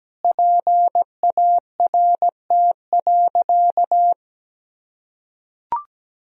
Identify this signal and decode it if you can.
Morse code, heard as a keyed tone, about 17 wpm, 700 Hz. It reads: PART.
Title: Sentences from Top 1000 Words - Review Part 1 - 17wpm